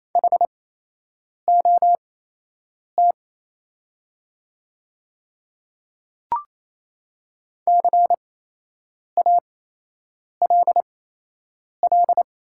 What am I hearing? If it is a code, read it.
Morse code, heard as a keyed tone, about 27 wpm, spaced as for 8 wpm, 690 Hz. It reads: HOT CALL
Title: 8x Character Spacing - Top 200 Words - 28wpm